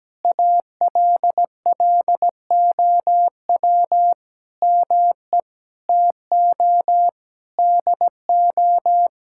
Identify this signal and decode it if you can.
Morse code, heard as a keyed tone, about 17 wpm, 700 Hz. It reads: ALLOW ME TO DO
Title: Sentences from Top 1000 Words - Review Part 5 - 17wpm